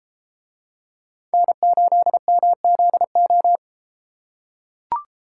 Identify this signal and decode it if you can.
Morse code, heard as a keyed tone, about 30 wpm, 700 Hz. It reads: N8MZO